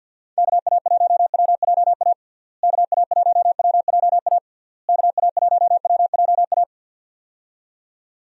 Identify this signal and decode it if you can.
Morse code, heard as a keyed tone, about 51 wpm, 700 Hz. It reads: KA1WJA KA1WJA KA1WJA
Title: USA Call Signs - 6 Characters - Repeated 3x - 50wpm